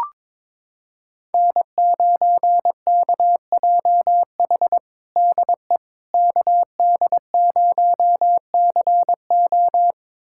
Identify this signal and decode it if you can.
Morse code, heard as a keyed tone, about 22 wpm, 700 Hz. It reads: N9KJH DE KD0CO
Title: USA Callsign DE Callsign - 5 Characters - 22wpm